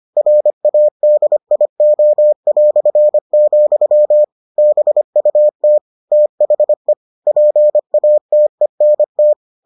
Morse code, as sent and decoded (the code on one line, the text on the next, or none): .-. .- -.. .. --- .-..-. --..-- / -... ..- - / - .... / .--. .- - . -. -
RADIO", BUT THE PATENT